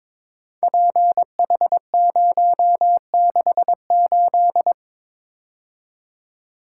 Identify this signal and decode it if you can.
Morse code, heard as a keyed tone, about 22 wpm, 700 Hz. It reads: PH068
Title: Code Groups - 5 Letter:Numbers - 22wpm